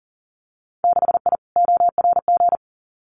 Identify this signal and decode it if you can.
Morse code, heard as a keyed tone, about 39 wpm, 700 Hz. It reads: 6I ORG
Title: Winter Field Day Exchanges Encore 40wpm